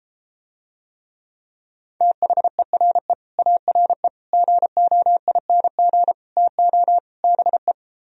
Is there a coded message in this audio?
Yes: Morse code from 2.0 to 7.7 s, heard as a keyed tone, about 33 wpm, 700 Hz. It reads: THERE ARE GOING TO BE